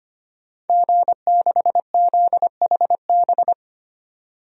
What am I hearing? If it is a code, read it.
Morse code, heard as a keyed tone, about 25 wpm, 700 Hz. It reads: G6ZHB
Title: DXCC-100 Call Signs - 5 Characters - 25wpm